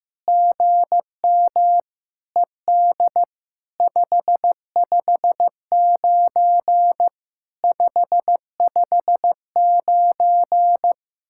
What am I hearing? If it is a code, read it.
Morse code, heard as a keyed tone, about 15 wpm, 700 Hz. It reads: GM ED 559 559
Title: SOTA QSO Practice 15wpm